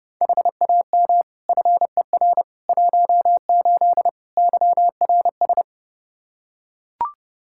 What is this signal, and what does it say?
Morse code, heard as a keyed tone, about 30 wpm, 700 Hz. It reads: HAM FER 18 YRS
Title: Practice QSO - Exchange 3 - More Info - 30wpm